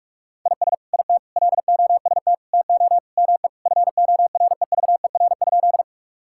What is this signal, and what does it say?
Morse code, heard as a keyed tone, about 44 wpm, 700 Hz. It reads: IS IT LOST TO ME FOREVER?
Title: Sentences from Top 1600 Words 45wpm